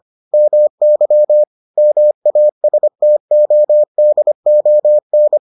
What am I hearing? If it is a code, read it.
Morse code, heard as a keyed tone, about 25 wpm, 600 Hz. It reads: MY MASTODON